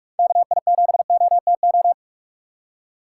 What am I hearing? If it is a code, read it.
Morse code, heard as a keyed tone, about 44 wpm, 700 Hz. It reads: KI7OTO
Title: USA Call Signs - 6x Character Spacing - 45wpm